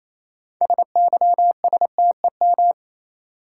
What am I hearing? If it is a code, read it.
Morse code, heard as a keyed tone, about 28 wpm, 700 Hz. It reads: SYSTEM
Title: Sentences from Top 1000 Words - Review Part 3 - New Word Drill - 28wpm